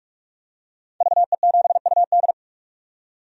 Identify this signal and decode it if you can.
Morse code, heard as a keyed tone, about 42 wpm, 700 Hz. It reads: VE7UD
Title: DXCC-100 Call Signs - 5 Characters - 45wpm